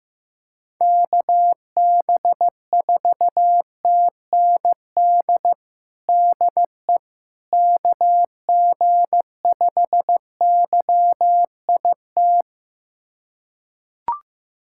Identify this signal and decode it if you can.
Morse code, heard as a keyed tone, about 15 wpm, 700 Hz. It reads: KB4TND DE KG5YIT
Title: USA Callsign DE Callsign - 6 Characters - 15wpm